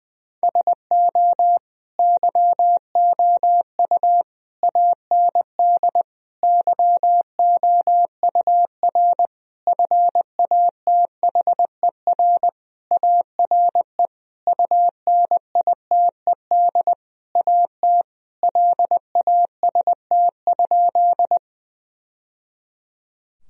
Morse code, heard as a keyed tone, about 20 wpm, 700 Hz. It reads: SO YOU AND YOUR FATHER ARE UNITED AT LAST?
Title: Sentences from Top 1000 Words - Review Part 4 - 20wpm